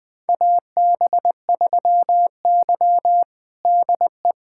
Morse code, heard as a keyed tone, about 20 wpm, 700 Hz. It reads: AB3Y DE
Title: USA Callsign DE Callsign - 4 Characters - 20wpm